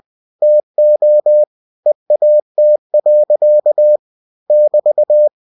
Morse code, heard as a keyed tone, about 20 wpm, 600 Hz. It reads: TO EAT. <BT>
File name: cw-books / winnie-the-pooh / winnie-20wpm-page-0077